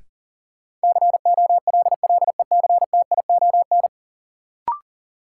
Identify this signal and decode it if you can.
Morse code, heard as a keyed tone, about 40 wpm, 700 Hz. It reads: COLLECTION
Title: Sentences from Top 1200 Words - New Word Drill - 40wpm